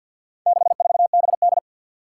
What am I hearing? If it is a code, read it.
Morse code, heard as a keyed tone, about 52 wpm, 700 Hz. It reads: 64BD
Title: Four Challenging Letter-Number Combinations 50wpm